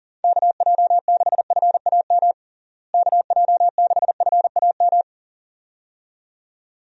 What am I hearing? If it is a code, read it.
Morse code, heard as a keyed tone, about 40 wpm, 700 Hz. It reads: KJ6FAM KJ6FAM
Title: USA Call Signs - 6 Characters - Repeated 2x - 40wpm